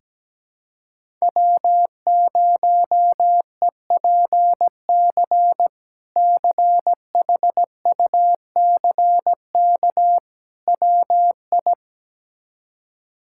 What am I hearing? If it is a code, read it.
Morse code, heard as a keyed tone, about 17 wpm, 700 Hz. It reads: W0EPC CHUCK WI
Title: K1USN SST QSO Practice - 17wpm